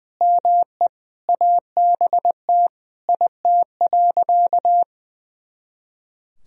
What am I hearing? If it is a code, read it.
Morse code, heard as a keyed tone, about 21 wpm, 700 Hz. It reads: ME ABT IT.
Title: Sentences from Top 200 Words and Abbreviations 20wpm